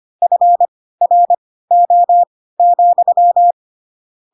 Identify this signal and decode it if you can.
Morse code, heard as a keyed tone, about 25 wpm, spaced as for 16 wpm, 700 Hz. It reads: FRO,
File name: Pinocchio / Pinocchio ch1 10 25 ik1hge